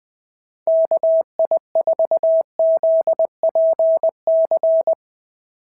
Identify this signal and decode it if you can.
Morse code, heard as a keyed tone, about 20 wpm, 650 Hz. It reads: KI4ZPC
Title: SOTA QSO Practice 20wpm